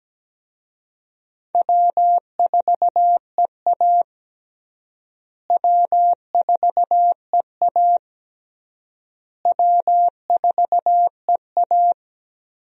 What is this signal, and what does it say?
Morse code, heard as a keyed tone, about 17 wpm, 700 Hz. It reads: W4EA W4EA W4EA
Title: USA Call Signs - 4 Characters - Repeated 3x - 17wpm